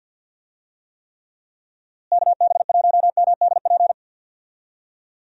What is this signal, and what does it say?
Morse code, heard as a keyed tone, about 51 wpm, 700 Hz. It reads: KB1MDP
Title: USA Call Signs - 6 Characters - Repeated 2x - 50wpm